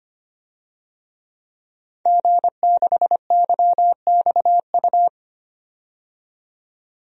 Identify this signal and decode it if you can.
Morse code, heard as a keyed tone, about 25 wpm, 700 Hz. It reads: G6YXU
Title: DXCC-100 Call Signs - 5 Characters - 25wpm